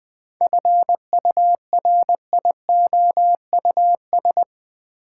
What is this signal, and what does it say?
Morse code, heard as a keyed tone, about 20 wpm, 700 Hz. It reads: FURIOUS
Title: 7 Letter Words - First 3 Letters Given - 20wpm